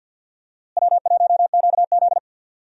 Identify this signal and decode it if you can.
Morse code, heard as a keyed tone, about 51 wpm, 700 Hz. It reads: W1QZ
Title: USA Call Signs - 4 Characters - Repeated 2x - 50wpm